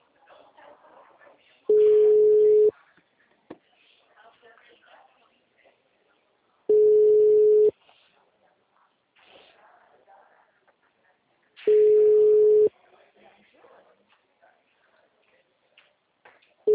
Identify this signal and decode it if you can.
Ringback tone (Europe, 425 Hz), with speech.